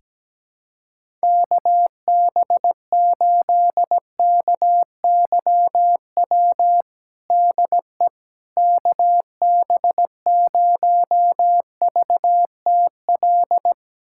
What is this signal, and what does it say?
Morse code, heard as a keyed tone, about 17 wpm, 700 Hz. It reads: KB8KYW DE KB0VTL
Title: USA Callsign DE Callsign 17wpm